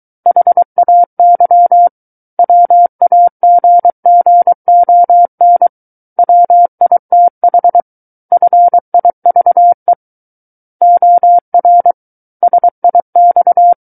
Morse code, heard as a keyed tone, about 23 wpm, 700 Hz. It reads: HAY WAGGON WITH FIVE OR SIX
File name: Book One / 14-Chapter 14 - 23 WPM